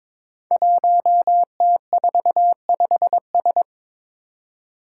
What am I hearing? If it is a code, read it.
Morse code, heard as a keyed tone, about 22 wpm, 700 Hz. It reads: 1T45S
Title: Five Challenging Letter-Number Combinations 22wpm